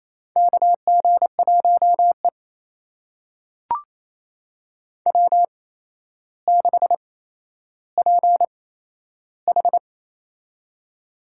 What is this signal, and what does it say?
Morse code, heard as a keyed tone, about 28 wpm, spaced as for 12 wpm, 700 Hz. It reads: KG1E W6PH
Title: USA Call Signs - 4 Characters - 8x Character Spacing - 28wpm